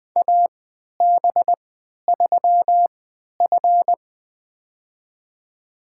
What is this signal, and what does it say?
Morse code, heard as a keyed tone, about 20 wpm, spaced as for 11 wpm, 700 Hz. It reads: AB3F
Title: USA Call Signs - 4 Characters - 3x Character Spacing - 20wpm